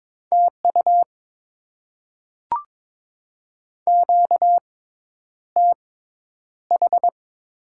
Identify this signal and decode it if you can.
Morse code, heard as a keyed tone, about 21 wpm, spaced as for 10 wpm, 690 Hz. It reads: TU QTH
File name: Top 20 QSO Elements - 6x Character Spacing - 22wpm